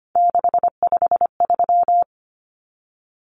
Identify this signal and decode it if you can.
Morse code, heard as a keyed tone, about 25 wpm, 700 Hz. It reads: 653
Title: Numbers - 3-Digit 25wpm